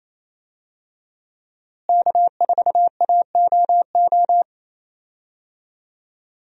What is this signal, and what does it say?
Morse code, heard as a keyed tone, about 28 wpm, 700 Hz. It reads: K4AOO
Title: USA Call Signs - 5 Characters - Repeated 3x - 28wpm